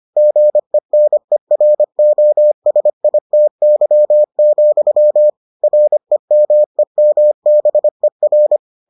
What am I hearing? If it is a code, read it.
Morse code, heard as a keyed tone, about 25 wpm, 600 Hz. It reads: GENEROSITY, REMEMBER